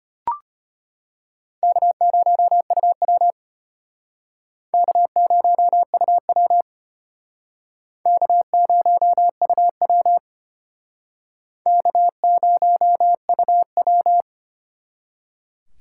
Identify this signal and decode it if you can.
Morse code, heard as a keyed tone, about 32 wpm, 700 Hz. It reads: K0UW K0UW K0UW K0UW